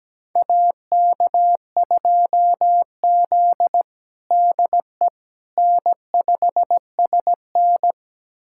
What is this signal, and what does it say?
Morse code, heard as a keyed tone, about 17 wpm, 700 Hz. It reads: AK2Z DE N5SN
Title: USA Callsign DE Callsign - 4 Characters - 17wpm